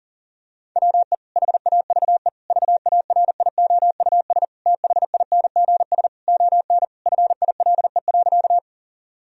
Morse code, heard as a keyed tone, about 40 wpm, 700 Hz. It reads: WE HAVE VARIOUS THINGS ON FILE.
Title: Sentences from Top 1000 Words - Review Part 4 - 40wpm